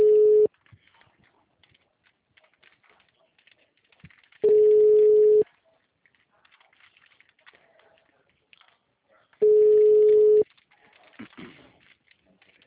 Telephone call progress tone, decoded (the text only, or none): ringback tone